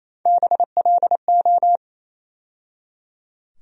Morse code, heard as a keyed tone, about 28 wpm, 700 Hz. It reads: BLO
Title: USA Call Sign Postfixes - 3 Letters 28wpm